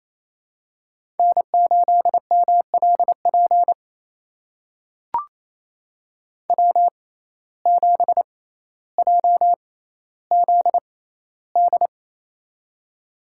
Morse code, heard as a keyed tone, about 28 wpm, spaced as for 14 wpm, 700 Hz. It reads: N8MLP W7JZD
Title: USA Call Signs - 5 Characters - 6x Character Spacing - 28wpm